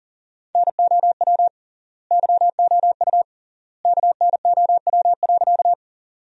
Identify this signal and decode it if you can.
Morse code, heard as a keyed tone, about 40 wpm, 700 Hz. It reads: NOW YOU KNOW.
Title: 3x Word-Spacing - Sentences from Top 100 Words - 40wpm